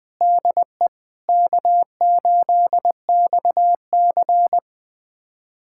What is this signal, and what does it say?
Morse code, heard as a keyed tone, about 20 wpm, 700 Hz. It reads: DE K8XC
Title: USA Callsign DE Callsign 20wpm